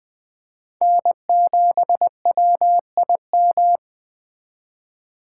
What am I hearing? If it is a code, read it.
Morse code, heard as a keyed tone, about 21 wpm, 700 Hz. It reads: N7WIM